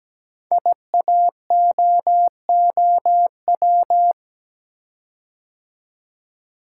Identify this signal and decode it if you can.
Morse code, heard as a keyed tone, about 17 wpm, 700 Hz. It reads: IAOOW
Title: Code Groups - 5 Letters - 17wpm